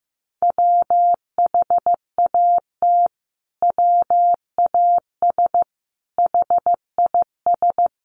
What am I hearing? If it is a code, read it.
Morse code, heard as a keyed tone, about 15 wpm, 700 Hz. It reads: WHAT WAS HIS
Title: Sentences from Top 1400 Words 15wpm